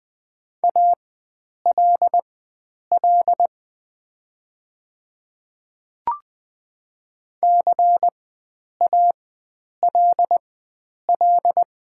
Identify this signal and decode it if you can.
Morse code, heard as a keyed tone, about 20 wpm, spaced as for 9 wpm, 690 Hz. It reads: ALL CALL
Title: 4x Character Spacing - Top 100 Words - 20wpm